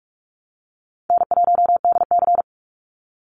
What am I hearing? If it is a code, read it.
Morse code, heard as a keyed tone, about 46 wpm, 700 Hz. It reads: N1DC
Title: Easy Call Signs - 45wpm